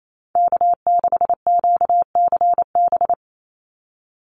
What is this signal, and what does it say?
Morse code, heard as a keyed tone, about 28 wpm, 700 Hz. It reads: K6QCB
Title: NA SOTA Chaser Callsigns 28wpm